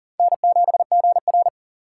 Morse code, heard as a keyed tone, about 40 wpm, 700 Hz. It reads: N7GR
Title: USA Call Signs - 4 Characters - 4x Character Spacing - 40wpm